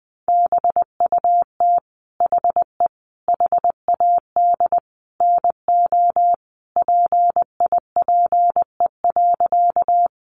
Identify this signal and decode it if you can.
Morse code, heard as a keyed tone, about 19 wpm, 700 Hz. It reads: BUT HE HAD NO PIPE.
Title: Sentences from Top 1500 Words 20wpm